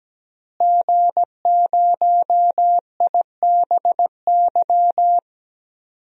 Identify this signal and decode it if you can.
Morse code, heard as a keyed tone, about 17 wpm, 700 Hz. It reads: G0IBY
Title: DXCC-100 Call Signs - 5 Characters - Repeated 2x - 17wpm